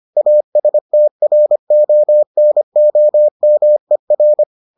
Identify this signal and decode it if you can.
Morse code, heard as a keyed tone, about 25 wpm, 600 Hz. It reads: ASTRONOMER